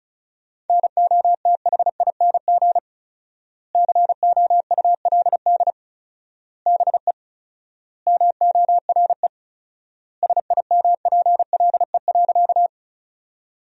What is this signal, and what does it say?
Morse code, heard as a keyed tone, about 34 wpm, 700 Hz. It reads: NOTHING COULD BE MORE SIMPLE.
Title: 4x Word-Spacing - Sentences from Top 500 Words - 35wpm